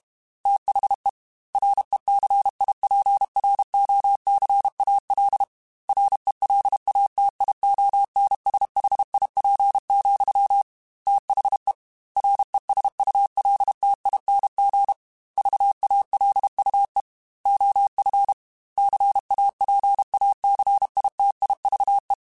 Morse code, heard as a keyed tone, about 32 wpm, 800 Hz. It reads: THE RECIPROCAL RELATIONSHIP, THE RESULTING VALUE OF CAPACITIVE
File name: Technical Subjects / Technical Subjects-mp3s / XC @30